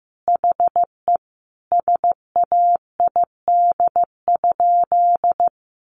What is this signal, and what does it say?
Morse code, heard as a keyed tone, about 15 wpm, 700 Hz. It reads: HE SAID?
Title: Sentences from Top 900 Words 15wpm